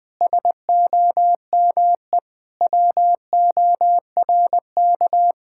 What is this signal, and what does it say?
Morse code, heard as a keyed tone, about 20 wpm, 700 Hz. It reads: SOME WORK
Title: Sentences from Top 1000 Words - Review Part 6 - 20wpm